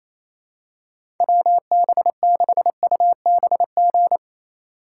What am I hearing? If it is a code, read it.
Morse code, heard as a keyed tone, about 28 wpm, 700 Hz. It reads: WB6UBG
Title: USA Call Signs - 6 Characters - 6x Character Spacing - 28wpm